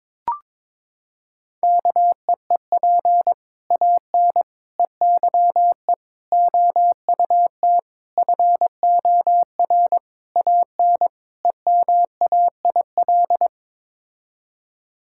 Morse code, heard as a keyed tone, about 22 wpm, 700 Hz. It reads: KEEP AN EYE OUT FOR AN EMAIL
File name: practice / 22wpm / Sets of 7 Words - 22wpm